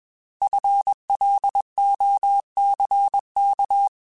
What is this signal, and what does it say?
Morse code, heard as a keyed tone, about 21 wpm, 800 Hz. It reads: FLOCK
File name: Alphabetical / Alphabetical mp3s / Fs @20